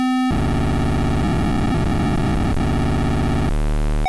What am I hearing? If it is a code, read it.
POCSAG pager data.